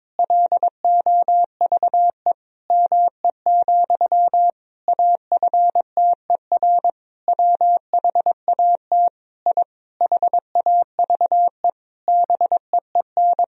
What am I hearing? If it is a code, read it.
Morse code, heard as a keyed tone, about 22 wpm, 700 Hz. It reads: LOVE ME, AFTER WHAT I HAVE BEEN
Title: Sentences from Top 1000 Words - Review Part 1 - 22wpm